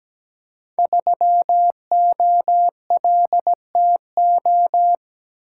Morse code, heard as a keyed tone, about 17 wpm, 700 Hz. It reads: 3OLTO
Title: Code Groups - 5 Letter:Numbers - 17wpm